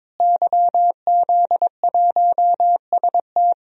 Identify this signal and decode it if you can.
Morse code, heard as a keyed tone, about 22 wpm, 700 Hz. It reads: YZ1ST
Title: DXCC-100 Call Signs - 5 Characters - Repeated 3x - 22wpm